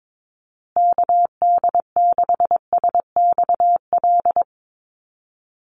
Morse code, heard as a keyed tone, about 22 wpm, 700 Hz. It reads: KD6SXL
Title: USA Call Signs Repeated 2x 22wpm